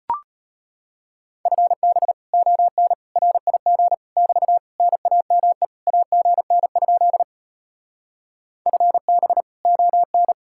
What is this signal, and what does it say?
Morse code, heard as a keyed tone, about 33 wpm, 700 Hz. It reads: FB ON RIG <BT> NAME AGN? FB ON